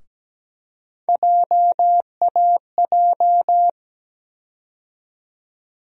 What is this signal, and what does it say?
Morse code, heard as a keyed tone, about 17 wpm, 700 Hz. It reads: JAJ
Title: USA Call Sign Postfixes - 3 Letters 17wpm